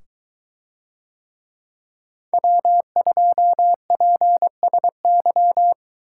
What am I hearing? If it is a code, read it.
Morse code, heard as a keyed tone, about 22 wpm, 700 Hz. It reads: W2PSY